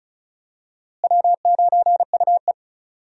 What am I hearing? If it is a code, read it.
Morse code, heard as a keyed tone, about 36 wpm, 700 Hz. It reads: W9UE
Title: USA Call Signs - 4 Characters - Repeated 3x - 35wpm